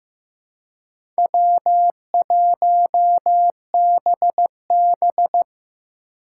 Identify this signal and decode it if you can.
Morse code, heard as a keyed tone, about 15 wpm, 700 Hz. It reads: W1BB
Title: USA Call Signs - 4 Characters - 2x Character Spacing - 15wpm